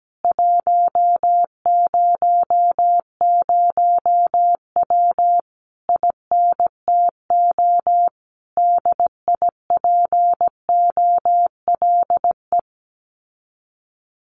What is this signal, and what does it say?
Morse code, heard as a keyed tone, about 17 wpm, 700 Hz. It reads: 100W INTO DIPOLE
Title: Common QSO Phrases Encore 17wpm